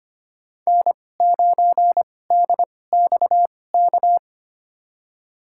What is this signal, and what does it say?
Morse code, heard as a keyed tone, about 25 wpm, spaced as for 18 wpm, 700 Hz. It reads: N9DXK
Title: USA Call Signs - 5 Characters - 2x Character Spacing - 25wpm